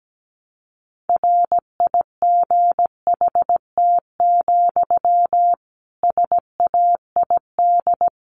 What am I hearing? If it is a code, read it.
Morse code, heard as a keyed tone, about 17 wpm, 700 Hz. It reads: RIGHT, SAID
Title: A Sound Of Thunder 17wpm